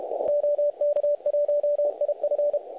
Morse code, heard as a keyed tone, about 32 wpm, 590 Hz. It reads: OK1IF
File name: truSDX / websdr recording 3541.9kHz